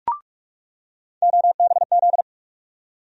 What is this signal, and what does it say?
Morse code, heard as a keyed tone, about 42 wpm, 700 Hz. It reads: OBZ
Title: USA Call Sign Postfixes - 3 Letters 45wpm